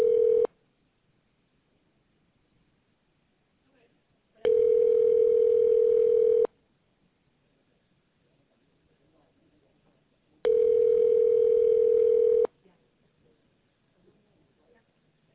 Ringback tone (North America), 440 + 480 Hz.